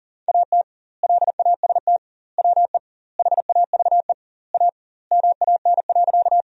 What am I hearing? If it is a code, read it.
Morse code, heard as a keyed tone, about 40 wpm, 700 Hz. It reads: AT LAST WE HAVE A MAN.
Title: 2x Word-Spacing - Sentences from Top 300 Words - 40wpm